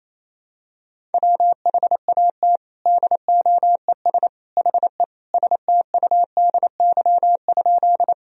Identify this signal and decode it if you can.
Morse code, heard as a keyed tone, about 28 wpm, 700 Hz. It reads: WHAT DOES HE STUDY?
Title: Sentences from Top 1000 Words - Review Part 1 - 28wpm